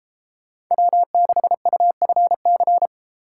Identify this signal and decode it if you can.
Morse code, heard as a keyed tone, about 31 wpm, 700 Hz. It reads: W6UFC